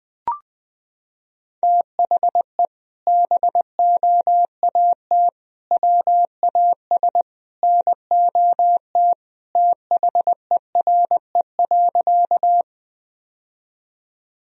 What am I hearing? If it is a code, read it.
Morse code, heard as a keyed tone, about 20 wpm, 700 Hz. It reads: THE BOAT WAS NOT THERE.
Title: Sentences from Top 1000 Words - Review Part 3 - 20wpm